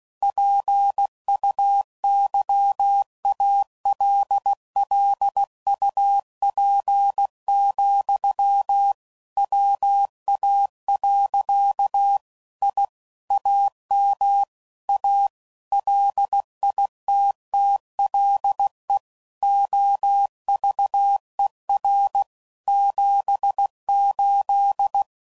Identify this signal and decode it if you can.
Morse code, heard as a keyed tone, about 16 wpm, 800 Hz. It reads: PUYALLUP, WA. I AM A LITTLE OVER 78
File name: CW Code Groups @15WPM / QSOs / kc7sac